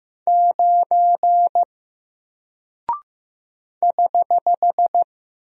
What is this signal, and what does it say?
Morse code, heard as a keyed tone, about 15 wpm, 700 Hz. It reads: 9 <HH>
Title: Single Character (Letter:Number:Punctuation:Prosign) 15wpm